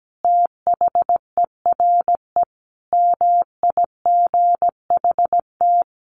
Morse code, heard as a keyed tone, about 17 wpm, 700 Hz. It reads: THERE MIGHT
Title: Sentences from Top 1400 Words 17wpm